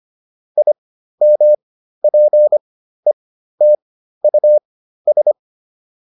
Morse code, heard as a keyed tone, about 25 wpm, spaced as for 13 wpm, 600 Hz. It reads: IMPETUS
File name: IM7 13